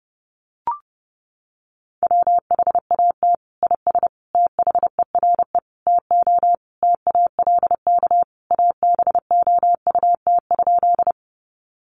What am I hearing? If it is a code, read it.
Morse code, heard as a keyed tone, about 30 wpm, 700 Hz. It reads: WHAT IS THERE TO TALK ABOUT?
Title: Sentences from Top 400 Words 30wpm